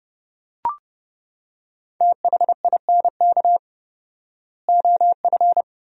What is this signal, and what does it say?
Morse code, heard as a keyed tone, about 32 wpm, 700 Hz. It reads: THINK OF
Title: 4x Word-Spacing - Sentences from Top 300 Words - 30wpm